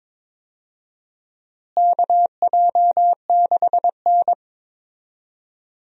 Morse code, heard as a keyed tone, about 22 wpm, 700 Hz. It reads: KJ6N
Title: USA Call Signs - 4 Characters - Repeated 3x - 22wpm